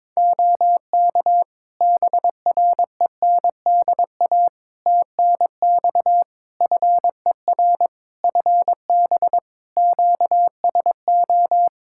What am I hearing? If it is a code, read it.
Morse code, heard as a keyed tone, about 22 wpm, 700 Hz. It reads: OK BRENDA TNX FER FB QSO
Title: Practice QSO - Exchange 4 - Ending - 22wpm